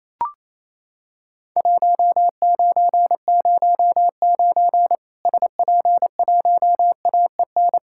Morse code, heard as a keyed tone, about 28 wpm, 700 Hz. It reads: 1909 SP1AEN
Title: Worked All Europe DX Contest - QTC Exchange - 28wpm